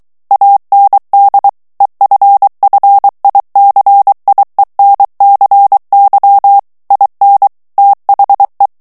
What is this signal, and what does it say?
Morse code, heard as a keyed tone, about 23 wpm, 800 Hz. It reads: AND EFFICIENCY IN THE
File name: ANGLAIS texte clair 3 vitesse 22 mots minute